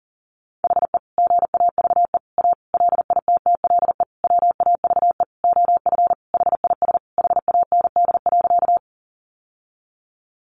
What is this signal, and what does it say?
Morse code, heard as a keyed tone, about 40 wpm, 700 Hz. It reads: HE GAVE A LITTLE WAVE OF HIS HAND.
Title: Sentences from Top 1400 Words 40wpm